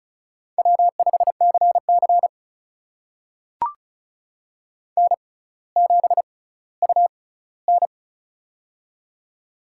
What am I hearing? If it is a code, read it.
Morse code, heard as a keyed tone, about 34 wpm, spaced as for 18 wpm, 700 Hz. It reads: W5CC N7UN
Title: USA Call Signs - 4 Characters - 6x Character Spacing - 35wpm